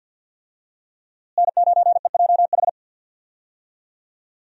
Morse code, heard as a keyed tone, about 52 wpm, 700 Hz. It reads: N9EJH
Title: USA Call Signs - 5 Characters - Repeated 3x - 50wpm